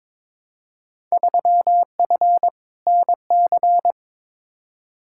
Morse code, heard as a keyed tone, about 22 wpm, 700 Hz. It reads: 3F NC
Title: Field Day Exchanges Encore - 22wpm